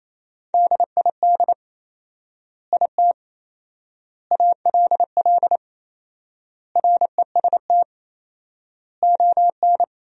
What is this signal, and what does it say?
Morse code, heard as a keyed tone, about 28 wpm, 700 Hz. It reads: DID IT ALL REST ON
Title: 4x Word-Spacing - Sentences from Top 500 Words - 28wpm